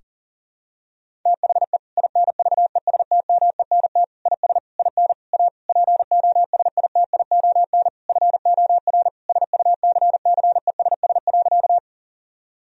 Morse code, heard as a keyed tone, about 37 wpm, 700 Hz. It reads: THE INVESTMENT IS IN A POSITION FOR SUCCESS.
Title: Sentences from Top 1000 Words - Review Part 4 - 40wpm